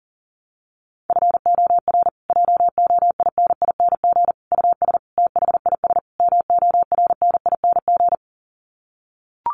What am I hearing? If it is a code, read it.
Morse code, heard as a keyed tone, about 40 wpm, 700 Hz. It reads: FOR JOINING US THIS MORNING
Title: Sets of 5 Words Encore 40wpm